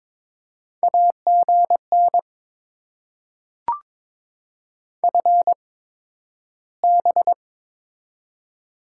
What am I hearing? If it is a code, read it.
Morse code, heard as a keyed tone, about 22 wpm, spaced as for 12 wpm, 700 Hz. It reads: AGN FB
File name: Top 20 QSO Elements - 8x Character Spacing - 22wpm